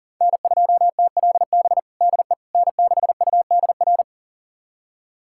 Morse code, heard as a keyed tone, about 37 wpm, 700 Hz. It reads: N2TLB DE N6UDR
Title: USA Callsign DE Callsign - 5 Characters - 40wpm